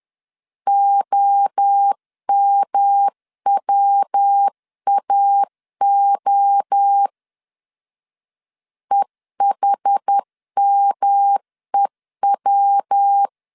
Morse code, heard as a keyed tone, about 11 wpm, 800 Hz. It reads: OMWAO EHMEW
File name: 10wpm